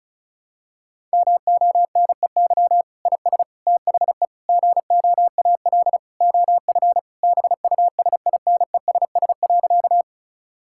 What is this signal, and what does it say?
Morse code, heard as a keyed tone, about 34 wpm, 700 Hz. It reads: MONEY IS THE GOAL OF BUSINESS.
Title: Sentences from Top 1000 Words - Review Part 5 - 35wpm